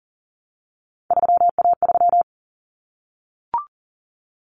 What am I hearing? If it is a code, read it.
Morse code, heard as a keyed tone, about 40 wpm, 700 Hz. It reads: 3A3
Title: Common Cut Numbers - 3-Digit 40wpm